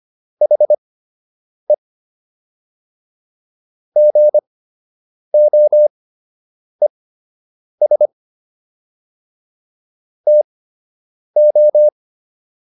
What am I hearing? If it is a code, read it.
Morse code, heard as a keyed tone, about 25 wpm, spaced as for 8 wpm, 600 Hz. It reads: HE GOES TO